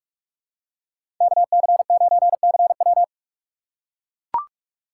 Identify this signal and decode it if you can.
Morse code, heard as a keyed tone, about 42 wpm, 700 Hz. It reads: KC9CW
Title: USA Call Signs - 5 Characters - 4x Character Spacing - 45wpm